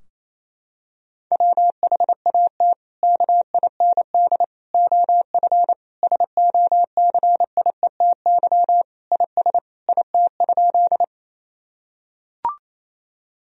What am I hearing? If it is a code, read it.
Morse code, heard as a keyed tone, about 28 wpm, 700 Hz. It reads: WHAT KIND OF SOCIETY IS IT?
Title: Sentences from Top 1000 Words - Review Part 4 - 28wpm